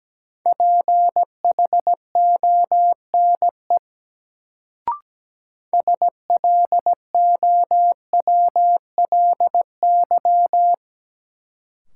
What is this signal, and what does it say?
Morse code, heard as a keyed tone, about 17 wpm, 700 Hz. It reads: PHONE SLOWLY
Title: Sentences from Top 1100 Words - New Word Drill - 17wpm